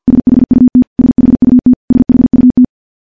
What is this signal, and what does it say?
Morse code, heard as a keyed tone, about 50 wpm, 250 Hz. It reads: SHUT SHUT SHUT
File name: morse / morse-training-50WPM